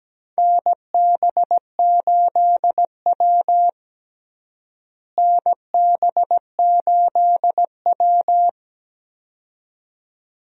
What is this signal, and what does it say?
Morse code, heard as a keyed tone, about 17 wpm, 700 Hz. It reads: NB8W NB8W
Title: USA Call Signs - 4 Characters - Repeated 2x - 17wpm